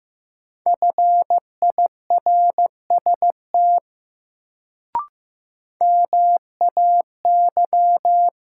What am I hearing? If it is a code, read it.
Morse code, heard as a keyed tone, about 15 wpm, 700 Hz. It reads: FIRST MAY
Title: Sentences from Top 100 Words - New Word Drill - 15wpm